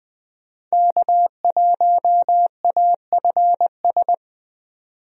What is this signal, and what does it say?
Morse code, heard as a keyed tone, about 20 wpm, 700 Hz. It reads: K1AFS
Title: USA Call Signs - 5 Characters - 20wpm